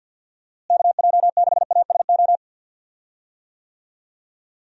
Morse code, heard as a keyed tone, about 50 wpm, 700 Hz. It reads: KJ6ASO
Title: USA Call Signs - 6 Characters - Repeated 3x - 50wpm